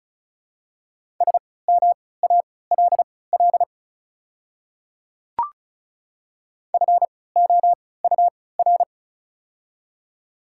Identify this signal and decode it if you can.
Morse code, heard as a keyed tone, about 36 wpm, spaced as for 20 wpm, 700 Hz. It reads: SMALL FOUR
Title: 3x Character Spacing - Top 300 Words - 35wpm